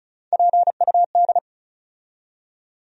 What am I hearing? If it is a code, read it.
Morse code, heard as a keyed tone, about 36 wpm, 700 Hz. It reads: PUD
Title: USA Call Sign Postfixes - 3 Letters 35wpm